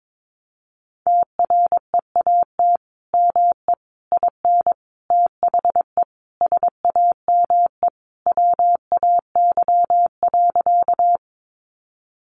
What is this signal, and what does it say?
Morse code, heard as a keyed tone, about 22 wpm, 700 Hz. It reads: TREAT ME IN THE SAME WAY.
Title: Sentences from Top 1300 Words 22wpm